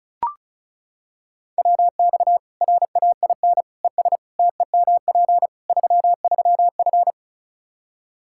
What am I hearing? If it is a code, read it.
Morse code, heard as a keyed tone, about 34 wpm, 700 Hz. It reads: WX RAIN ES TEMP 33F
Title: Practice QSO - Exchange 2 - Info - 35wpm